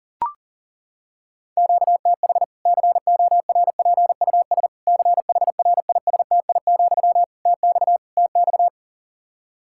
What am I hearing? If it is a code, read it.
Morse code, heard as a keyed tone, about 40 wpm, 700 Hz. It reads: QTH CORPUS CHRISTI, TX TX
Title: K1USN SST QSO Practice 40wpm